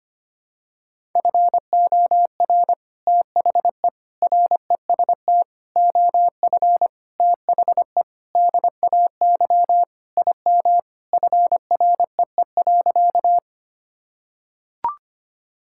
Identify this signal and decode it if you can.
Morse code, heard as a keyed tone, about 25 wpm, 700 Hz. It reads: FOR THE REST OF THE DAY IM FREE.
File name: practice / 25wpm / Sentences with 3-Word Phrases - 25wpm